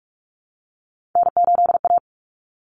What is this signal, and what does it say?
Morse code, heard as a keyed tone, about 44 wpm, 700 Hz. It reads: N8A
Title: Common Cut Numbers - 3-Digit 45wpm